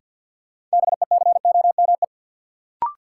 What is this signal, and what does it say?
Morse code, heard as a keyed tone, about 50 wpm, 700 Hz. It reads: BECOME
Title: Sets of 5 Words Encore - New Word Drill - 50wpm